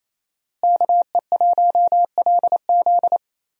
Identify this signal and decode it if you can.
Morse code, heard as a keyed tone, about 28 wpm, 700 Hz. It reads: KE1LZ
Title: USA Call Signs - 5 Characters - Repeated 2x - 28wpm